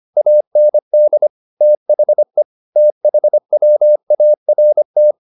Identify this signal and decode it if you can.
Morse code, heard as a keyed tone, about 25 wpm, 600 Hz. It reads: AND THE THWART
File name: the old man and the sea part 1 25wpm